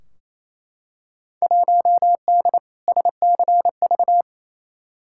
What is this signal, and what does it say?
Morse code, heard as a keyed tone, about 28 wpm, 700 Hz. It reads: 1D SCV
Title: Field Day Exchanges Encore - 28wpm